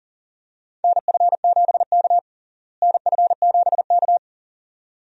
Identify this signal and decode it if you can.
Morse code, heard as a keyed tone, about 40 wpm, 700 Hz. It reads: NF7K NF7K
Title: USA Call Signs - 4 Characters - Repeated 2x - 40wpm